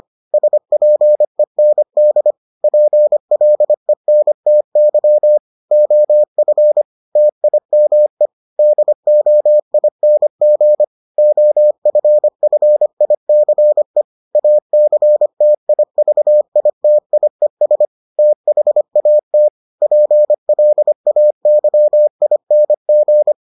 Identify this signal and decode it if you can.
Morse code, heard as a keyed tone, about 25 wpm, 600 Hz. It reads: SPEND PLENTY OF TIME DOING OFFICE ACTIVITIES THAT PLAYING